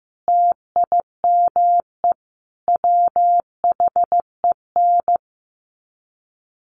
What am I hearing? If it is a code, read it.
Morse code, heard as a keyed tone, about 15 wpm, 700 Hz. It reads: TIME WHEN
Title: Sets of 5 Words 15wpm